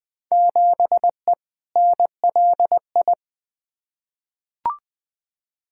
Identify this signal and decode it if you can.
Morse code, heard as a keyed tone, about 20 wpm, 700 Hz. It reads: 7E NLI
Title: Field Day Exchanges Encore - 20wpm